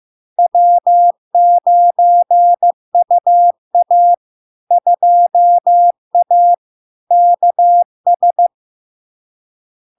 Morse code, heard as a keyed tone, about 15 wpm, 700 Hz. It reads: W9UA 2A KS
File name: LARK-FD-2022-Log-in-CW-15wpm-700-Hz